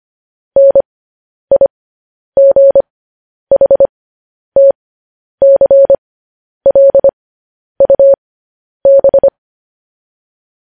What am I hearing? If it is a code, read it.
Morse code, heard as a keyed tone, about 25 wpm, spaced as for 10 wpm, 550 Hz. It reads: NIGHTCLUB
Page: news headlines in morse code at 10 wpm